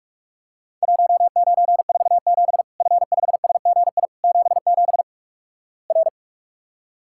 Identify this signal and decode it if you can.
Morse code, heard as a keyed tone, about 46 wpm, 700 Hz. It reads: 1947 F5SGI 77 R
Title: Worked All Europe DX Contest - QTC Exchange - 45wpm